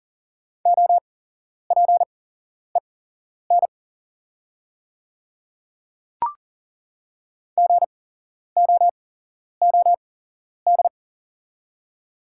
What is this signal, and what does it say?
Morse code, heard as a keyed tone, about 38 wpm, spaced as for 11 wpm, 700 Hz. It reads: OPEN GOOD
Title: 8x Character Spacing - Top 300 Words - 40wpm